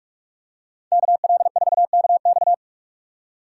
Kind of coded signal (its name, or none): Morse code